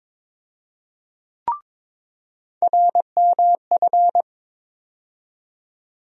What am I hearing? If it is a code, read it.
Morse code, heard as a keyed tone, about 21 wpm, 700 Hz. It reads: RMF
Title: USA Call Sign Postfixes - 3 Letters 22wpm